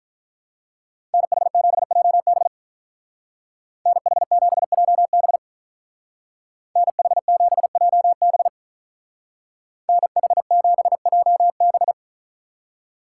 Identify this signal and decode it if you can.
Morse code, heard as a keyed tone, about 47 wpm, 700 Hz. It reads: NH7JB NH7JB NH7JB NH7JB